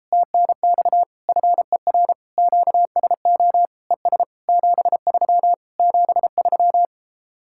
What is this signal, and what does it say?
Morse code, heard as a keyed tone, about 38 wpm, 700 Hz. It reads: TNX FER QSO ES 73 73